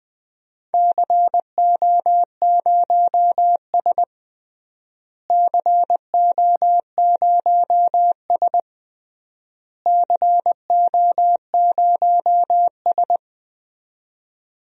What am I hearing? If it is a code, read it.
Morse code, heard as a keyed tone, about 21 wpm, 700 Hz. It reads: CO0S CO0S CO0S
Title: DXCC-100 Call Signs - 4 Characters - Repeated 3x - 20wpm